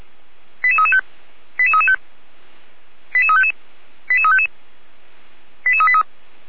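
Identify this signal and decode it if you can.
Five-tone selective call (ZVEI), heard as quick runs of five tones, tones of about 70 ms. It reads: 88335 88335 88355 88355 88383
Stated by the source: Five-tone ZVEI selective call 88367, received on an SDR.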